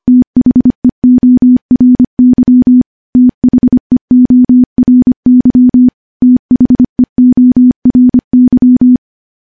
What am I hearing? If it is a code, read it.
Morse code, heard as a keyed tone, about 25 wpm, 250 Hz. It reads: THEORY THEORY THEORY
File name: morse / morse-training-25WPM